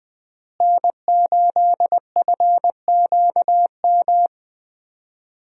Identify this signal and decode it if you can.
Morse code, heard as a keyed tone, about 21 wpm, 700 Hz. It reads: N8FQM
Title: USA Call Signs - 5 Characters - Repeated 2x - 20wpm